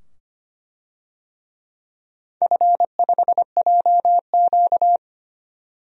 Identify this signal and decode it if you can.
Morse code, heard as a keyed tone, about 25 wpm, 700 Hz. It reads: F5JQ
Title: DXCC-100 Call Signs - 4 Characters - 25wpm